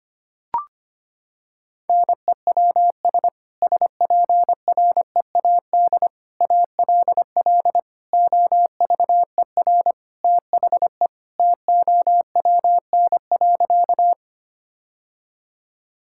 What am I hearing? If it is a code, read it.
Morse code, heard as a keyed tone, about 25 wpm, 700 Hz. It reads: NEWS SPREAD ALL OVER THE TOWN.
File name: practice / 25wpm / Sentences with 3-Word Phrases - 25wpm